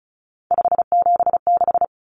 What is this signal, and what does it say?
Morse code, heard as a keyed tone, about 34 wpm, 700 Hz. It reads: 576
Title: Numbers - 3-Digit 35wpm